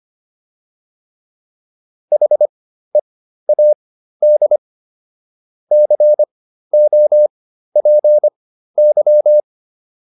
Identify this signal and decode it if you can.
Morse code, heard as a keyed tone, about 25 wpm, spaced as for 13 wpm, 600 Hz. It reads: HEAD COPY